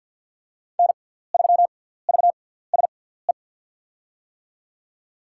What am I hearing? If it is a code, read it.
Morse code, heard as a keyed tone, about 47 wpm, spaced as for 17 wpm, 700 Hz. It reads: N3VSE
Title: USA Call Signs - 5 Characters - 6x Character Spacing - 50wpm